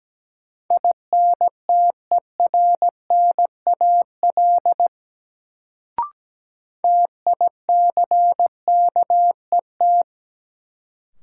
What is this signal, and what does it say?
Morse code, heard as a keyed tone, about 17 wpm, 700 Hz. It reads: INTERNAL TICKET
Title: Sentences from Top 1400 Words - New Word Drill - 17wpm